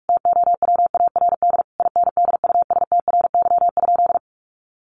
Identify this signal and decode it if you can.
Morse code, heard as a keyed tone, about 44 wpm, 700 Hz. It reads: TOWARD INDUSTRY?
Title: Sentences from Top 1300 Words 45wpm